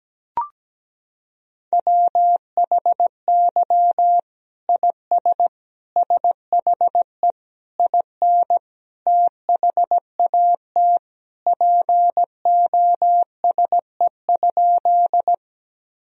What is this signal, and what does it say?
Morse code, heard as a keyed tone, about 17 wpm, 700 Hz. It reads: WHY IS SHE IN THAT POSE?
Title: Sentences from Top 1000 Words - Review Part 2 - 17wpm